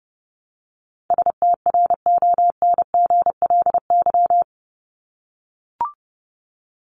Morse code, heard as a keyed tone, about 30 wpm, 700 Hz. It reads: STRONGLY
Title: LY Suffix - Top 100 Words with Base Words 30wpm